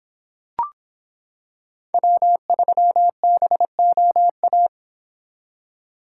Morse code, heard as a keyed tone, about 26 wpm, 700 Hz. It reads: W3BOA